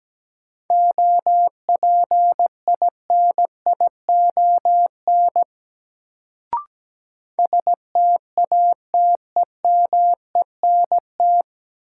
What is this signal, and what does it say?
Morse code, heard as a keyed tone, about 17 wpm, 700 Hz. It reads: OPINION STATEMENT
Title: Sentences from Top 1200 Words - New Word Drill - 17wpm